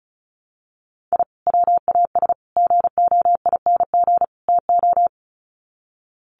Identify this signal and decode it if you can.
Morse code, heard as a keyed tone, about 34 wpm, 700 Hz. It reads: I WAS GOING TO
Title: Sets of 4 Words 35wpm